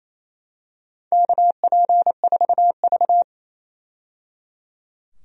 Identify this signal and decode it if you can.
Morse code, heard as a keyed tone, about 28 wpm, 700 Hz. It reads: KP4V
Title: DXCC-100 Call Signs - 4 Characters - 28wpm